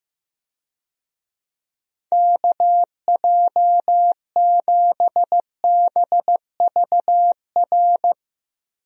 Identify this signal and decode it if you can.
Morse code, heard as a keyed tone, about 15 wpm, 700 Hz. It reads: KJ7BVR
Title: USA Call Signs - 6 Characters - 15wpm